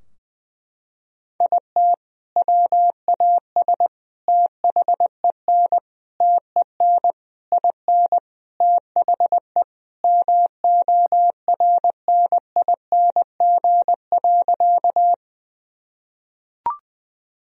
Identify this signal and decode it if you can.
Morse code, heard as a keyed tone, about 20 wpm, 700 Hz. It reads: IT WAS THEN TEN IN THE MORNING.
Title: Sentences from Top 1000 Words - Review Part 2 - 20wpm